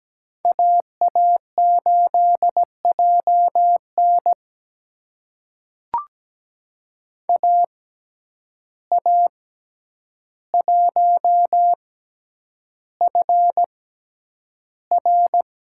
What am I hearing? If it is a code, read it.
Morse code, heard as a keyed tone, about 17 wpm, spaced as for 9 wpm, 700 Hz. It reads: AA8JN AA1FR
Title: USA Call Signs - 5 Characters - 6x Character Spacing - 17wpm